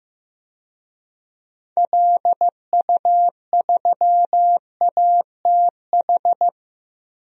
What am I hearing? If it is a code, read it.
Morse code, heard as a keyed tone, about 15 wpm, 700 Hz. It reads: LU3ATH
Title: DXCC-100 Call Signs - 6 Characters - 15wpm